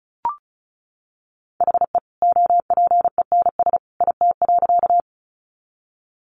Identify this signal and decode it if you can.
Morse code, heard as a keyed tone, about 34 wpm, 700 Hz. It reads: HE OPENS IT.
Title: Additional Easy Phrases 35wpm